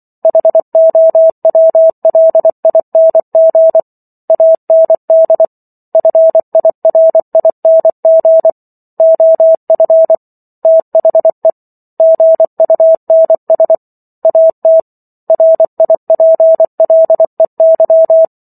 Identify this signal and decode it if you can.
Morse code, heard as a keyed tone, about 24 wpm, 650 Hz. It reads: HOWLING AND FIRING OF THE GUNS AT RIPLEY